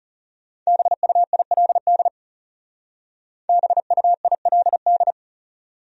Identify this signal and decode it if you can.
Morse code, heard as a keyed tone, about 37 wpm, 700 Hz. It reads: BUILD BUILD
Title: Speed-Racing Top 200 Words 30wpm